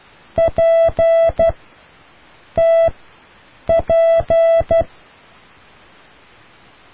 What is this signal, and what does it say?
Morse code, heard as a keyed tone, about 12 wpm, 660 Hz. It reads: P T P